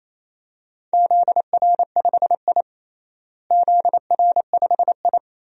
Morse code, heard as a keyed tone, about 28 wpm, 700 Hz. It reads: ZR5I ZR5I
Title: DXCC-100 Call Signs - 4 Characters - Repeated 2x - 28wpm